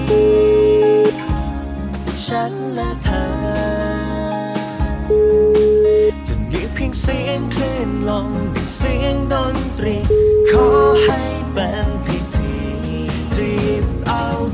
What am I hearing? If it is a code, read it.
Ringback tone (Europe, 425 Hz), with noise.